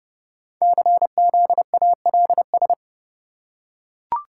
Morse code, heard as a keyed tone, about 28 wpm, 700 Hz. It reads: CZALS
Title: Code Groups - 5 Letters - 30wpm